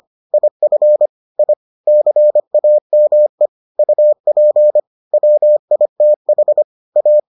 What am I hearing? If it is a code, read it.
Morse code, heard as a keyed tone, about 25 wpm, 600 Hz. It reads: IF I CAME UP WITH A